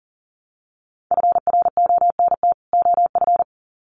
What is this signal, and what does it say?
Morse code, heard as a keyed tone, about 39 wpm, 700 Hz. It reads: FRONT OF
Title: Sets of 2 Words 40wpm